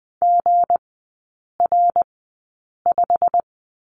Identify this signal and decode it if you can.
Morse code, heard as a keyed tone, about 20 wpm, spaced as for 8 wpm, 690 Hz. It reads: GR5
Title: 159 Lesson 26e - Letter g - Three Characters 20wpm